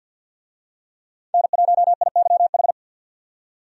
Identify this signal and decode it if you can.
Morse code, heard as a keyed tone, about 53 wpm, 700 Hz. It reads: N1IYH
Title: USA Call Signs - 5 Characters - 50wpm